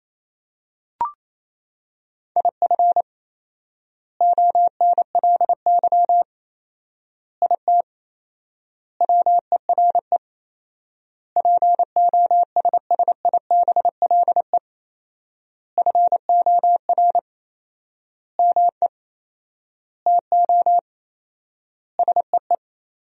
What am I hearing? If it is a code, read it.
Morse code, heard as a keyed tone, about 28 wpm, 700 Hz. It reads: IF ONLY IT WERE POSSIBLE FOR ME TO SEE
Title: 4x Word-Spacing - Sentences from Top 500 Words - 28wpm